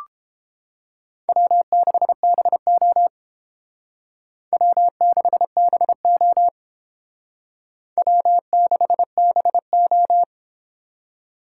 Morse code, heard as a keyed tone, about 31 wpm, 700 Hz. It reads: W6BO W6BO W6BO